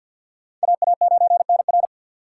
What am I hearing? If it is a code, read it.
Morse code, heard as a keyed tone, about 47 wpm, 700 Hz. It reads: AA9NR
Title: USA Call Signs - 3x Character Spacing - 50wpm